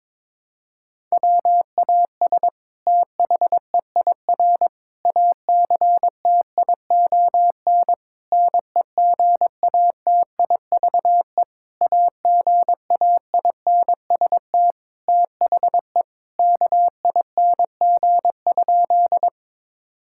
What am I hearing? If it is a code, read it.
Morse code, heard as a keyed tone, about 22 wpm, 700 Hz. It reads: WAS THEIR ACTION NEGATIVE AGAINST THE KING?
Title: Sentences from Top 1000 Words - Review Part 6 - 22wpm